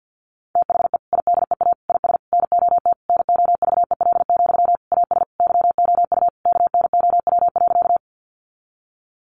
Morse code, heard as a keyed tone, about 50 wpm, 700 Hz. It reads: THE IDEA IS NOT NOVEL, AS YOU KNOW.
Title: Sentences from Top 1300 Words 50wpm